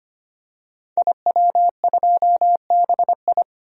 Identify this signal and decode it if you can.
Morse code, heard as a keyed tone, about 25 wpm, 700 Hz. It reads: IW2BI
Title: DXCC-100 Call Signs - 5 Characters - 25wpm